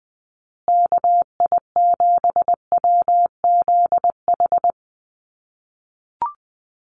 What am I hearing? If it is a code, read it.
Morse code, heard as a keyed tone, about 20 wpm, 700 Hz. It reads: KI7WZH
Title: USA Call Signs Repeated 2x 20wpm